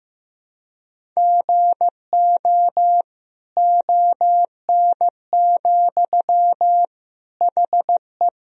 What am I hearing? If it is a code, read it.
Morse code, heard as a keyed tone, about 15 wpm, 700 Hz. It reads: GO ON, HE
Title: Sentences from Top 100 Words and Abbreviations 15wpm